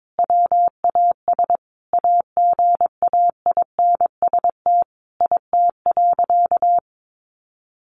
Morse code, heard as a keyed tone, about 22 wpm, 700 Hz. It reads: WAS AGAINST IT.
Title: Sentences from Top 1500 Words 22wpm